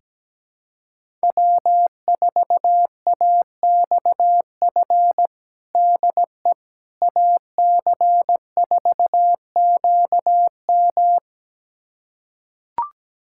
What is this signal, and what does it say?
Morse code, heard as a keyed tone, about 17 wpm, 700 Hz. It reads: W4AXF DE AC4QM
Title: USA Callsign DE Callsign - 5 Characters - 17wpm